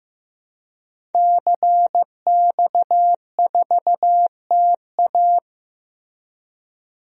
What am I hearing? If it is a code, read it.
Morse code, heard as a keyed tone, about 15 wpm, 700 Hz. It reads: CX4TA
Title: DXCC-100 Call Signs - 5 Characters - Repeated 3x - 15wpm